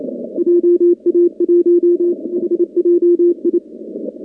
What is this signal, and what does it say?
Morse code, heard as a keyed tone, about 28 wpm, 350 Hz. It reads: JA1HJI
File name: contests / 2019-cw-wpx-kt5j-k5pi- -2330 0-11